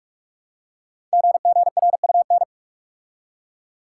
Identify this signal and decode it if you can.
Morse code, heard as a keyed tone, about 44 wpm, 700 Hz. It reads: GGRUN
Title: Code Groups - 5 Letters - 45wpm